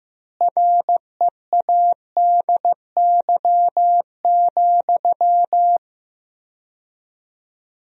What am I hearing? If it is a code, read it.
Morse code, heard as a keyed tone, about 15 wpm, 700 Hz. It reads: READY,
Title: 4x Word-Spacing - Sentences from Top 500 Words - 15wpm